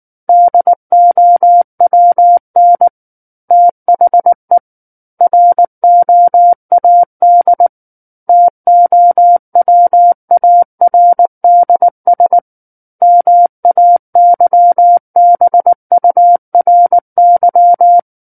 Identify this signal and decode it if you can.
Morse code, heard as a keyed tone, about 19 wpm, 700 Hz. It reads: DOWN THE ROAD TOWARDS MAYBURY
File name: Book One / 10-Chapter 10 - 19 WPM